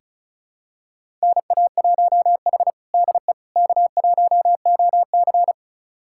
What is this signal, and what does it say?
Morse code, heard as a keyed tone, about 36 wpm, 700 Hz. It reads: NA1H DE K1OC
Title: USA Callsign DE Callsign - 4 Characters - 35wpm